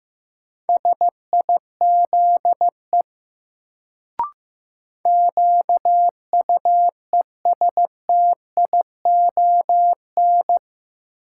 Morse code, heard as a keyed tone, about 15 wpm, 700 Hz. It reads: SIZE QUESTION
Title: Sentences from Top 400 Words - New Word Drill - 15wpm